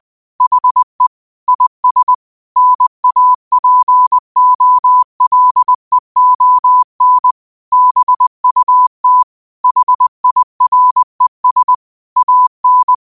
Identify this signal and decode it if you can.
Morse code, heard as a keyed tone, about 20 wpm, 1000 Hz. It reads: HE IS NAPOLEON BUT HIRES AN
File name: qotdfiles / 20wpm